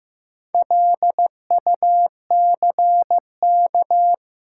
Morse code, heard as a keyed tone, about 15 wpm, 700 Hz. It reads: LUCK
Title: Sentences from Top 1500 Words - New Word Drill Alt - 15wpm